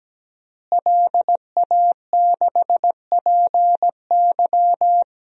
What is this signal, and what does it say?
Morse code, heard as a keyed tone, about 17 wpm, 700 Hz. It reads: LA6PY
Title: DXCC-100 Call Signs - 5 Characters - Repeated 2x - 17wpm